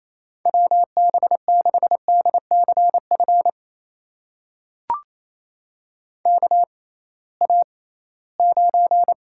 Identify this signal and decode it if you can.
Morse code, heard as a keyed tone, about 28 wpm, spaced as for 18 wpm, 700 Hz. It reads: WB6DCF KA9
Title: USA Call Signs - 6 Characters - 6x Character Spacing - 28wpm